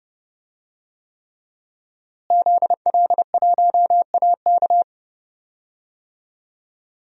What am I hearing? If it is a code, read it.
Morse code, heard as a keyed tone, about 30 wpm, 700 Hz. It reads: ZL1AK
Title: DXCC-100 Call Signs - 5 Characters - Repeated 3x - 30wpm